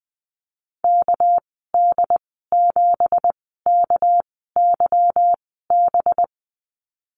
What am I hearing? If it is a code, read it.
Morse code, heard as a keyed tone, about 20 wpm, spaced as for 14 wpm, 700 Hz. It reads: KD7KYB
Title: 200 Lesson 32g - Letter k - Call Signs 20wpm